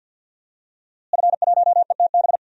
Morse code, heard as a keyed tone, about 52 wpm, 700 Hz. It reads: F1ETB
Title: DXCC-100 Call Signs - 5 Characters - Repeated 3x - 50wpm